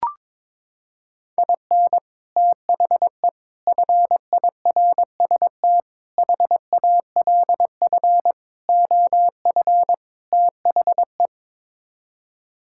Morse code, heard as a keyed tone, about 22 wpm, 700 Hz. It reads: IN THE FIRST HALF OF THE
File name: practice / 22wpm / Sets of 6 Words - 22wpm